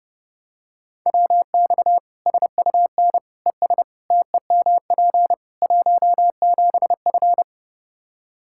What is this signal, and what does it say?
Morse code, heard as a keyed tone, about 30 wpm, 700 Hz. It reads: WX SUN ES TEMP 17F
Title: K1USN SST QSO Practice 30wpm